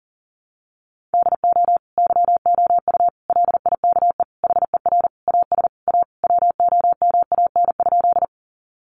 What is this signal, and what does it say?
Morse code, heard as a keyed tone, about 37 wpm, 700 Hz. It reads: DO YOU LIKE HER AS A WOMAN?
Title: Sentences from Top 1100 Words 40wpm